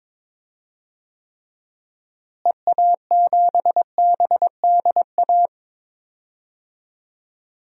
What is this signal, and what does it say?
Morse code, heard as a keyed tone, about 21 wpm, 700 Hz. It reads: EA7BDA
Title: DXCC-100 Call Signs - 6 Characters - Repeated 2x - 22wpm